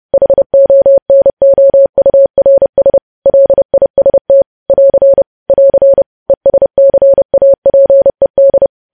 Morse code, heard as a keyed tone, about 30 wpm, 550 Hz. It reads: HONOURS LIST <AR> <AR> ESCAPED
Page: news headlines in morse code at 30 wpm